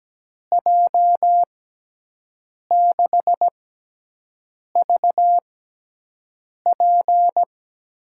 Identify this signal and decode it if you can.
Morse code, heard as a keyed tone, about 17 wpm, 700 Hz. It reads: J 6 V P
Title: USA Call Signs - 6 Characters - 6x Character Spacing - 17wpm